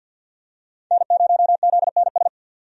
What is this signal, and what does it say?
Morse code, heard as a keyed tone, about 50 wpm, 700 Hz. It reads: N0ZNS
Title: USA Call Signs - 5 Characters - 50wpm